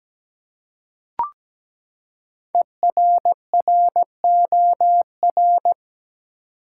Morse code, heard as a keyed tone, about 17 wpm, 700 Hz. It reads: ERROR
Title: Five Letter Words - 17wpm